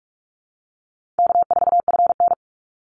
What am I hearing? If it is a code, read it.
Morse code, heard as a keyed tone, about 44 wpm, 700 Hz. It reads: K4FN
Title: Call Signs Repeated 2x - 45wpm